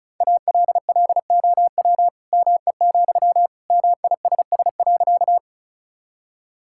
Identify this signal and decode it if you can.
Morse code, heard as a keyed tone, about 36 wpm, 700 Hz. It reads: ALLOW ME, MISS.
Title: Sentences from Top 1000 Words - Review Part 5 - 35wpm